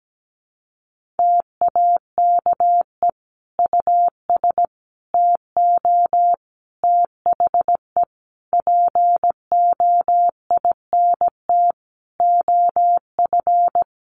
Morse code, heard as a keyed tone, about 17 wpm, 700 Hz. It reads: TAKE US TO THE POINT OF
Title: Sentences from Top 1500 Words 17wpm